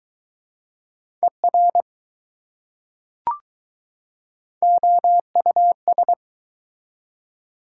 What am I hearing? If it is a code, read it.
Morse code, heard as a keyed tone, about 21 wpm, 700 Hz. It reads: ER OUS